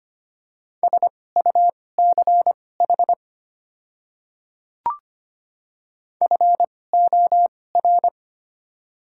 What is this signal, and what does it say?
Morse code, heard as a keyed tone, about 25 wpm, spaced as for 18 wpm, 700 Hz. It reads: SUCH FOR
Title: 2x Character Spacing - Top 200 Words - 25wpm